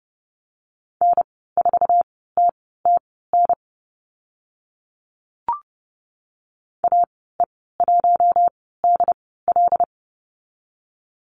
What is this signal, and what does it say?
Morse code, heard as a keyed tone, about 28 wpm, spaced as for 17 wpm, 700 Hz. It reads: N4TTN AE1DL
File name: Lesson 14g - Number 2 - Call Signs 30wpm